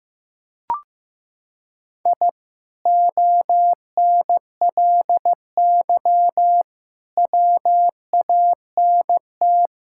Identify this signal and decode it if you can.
Morse code, heard as a keyed tone, about 16 wpm, 700 Hz. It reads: I ONLY WANT
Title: Sentences from Top 1000 Words - Review Part 6 - 15wpm